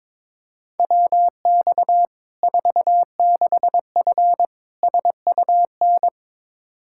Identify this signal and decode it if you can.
Morse code, heard as a keyed tone, about 21 wpm, 700 Hz. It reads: WX 46F SUN
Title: Practice QSO - Exchange 2 - Info - 22wpm